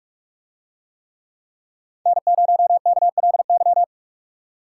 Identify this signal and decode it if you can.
Morse code, heard as a keyed tone, about 47 wpm, 700 Hz. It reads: N0KLY